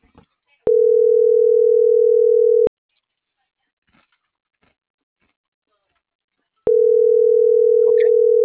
Ringback tone (North America), 440 + 480 Hz.